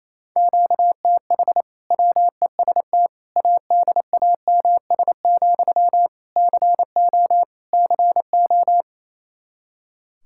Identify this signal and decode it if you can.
Morse code, heard as a keyed tone, about 28 wpm, 700 Hz. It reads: QTH WEST ADAMS, CO CO
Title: Practice QSO - Exchange 1 - Beginning - 28wpm